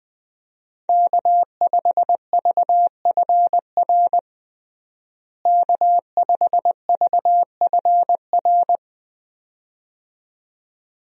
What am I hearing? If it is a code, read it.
Morse code, heard as a keyed tone, about 21 wpm, 700 Hz. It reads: K5VFR K5VFR
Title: USA Call Signs - 5 Characters - Repeated 2x - 20wpm